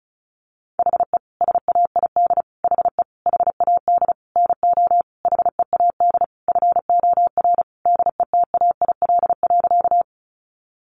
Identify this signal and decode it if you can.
Morse code, heard as a keyed tone, about 34 wpm, 700 Hz. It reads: HE SAID HE HAD NO HEAD FOR DETAIL.
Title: Sentences from Top 1500 Words 35wpm